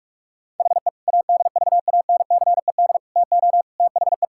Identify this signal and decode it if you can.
Morse code, heard as a keyed tone, about 41 wpm, 700 Hz. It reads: HE ADVANCED TO THE
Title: Sentences from Top 1600 Words 45wpm